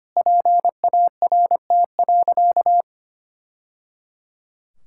Morse code, heard as a keyed tone, about 25 wpm, 700 Hz. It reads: PART.
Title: Sentences from Top 1000 Words - Review Part 5 - 25wpm